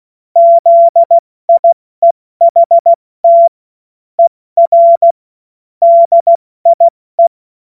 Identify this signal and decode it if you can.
Morse code, heard as a keyed tone, about 16 wpm, 680 Hz. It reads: ZIEHT ER DIE
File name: Morsekurs / Witze / Witz 02. Dezember 2025 70 F80